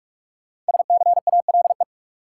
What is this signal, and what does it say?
Morse code, heard as a keyed tone, about 46 wpm, 700 Hz. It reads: SCALE
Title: Sentences from Top 1000 Words - Review Part 4 - New Word Drill - 45wpm